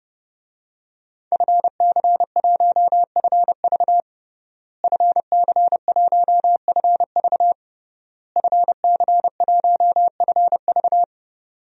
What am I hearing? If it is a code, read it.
Morse code, heard as a keyed tone, about 30 wpm, 700 Hz. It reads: FC1FV FC1FV FC1FV
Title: DXCC-100 Call Signs - 5 Characters - Repeated 3x - 30wpm